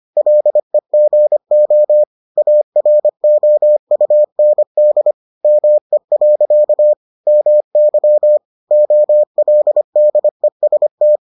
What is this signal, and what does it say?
Morse code, heard as a keyed tone, about 25 wpm, 600 Hz. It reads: LEGO AROUND ME. MY OLDEST